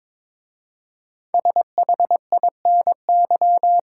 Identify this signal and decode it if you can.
Morse code, heard as a keyed tone, about 22 wpm, 700 Hz. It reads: SHINY
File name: 5 Letter Words - First 3 Letters Given - 22wpm